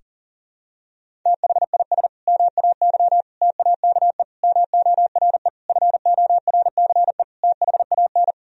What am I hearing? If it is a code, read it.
Morse code, heard as a keyed tone, about 40 wpm, 700 Hz. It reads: THIS MAY TAKE MORE FORCE THAN
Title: Sentences from Top 1000 Words - Review Part 3 - 40wpm